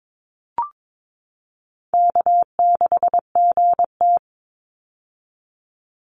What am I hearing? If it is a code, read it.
Morse code, heard as a keyed tone, about 22 wpm, 700 Hz. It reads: K6GT
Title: Easy Call Signs - 22wpm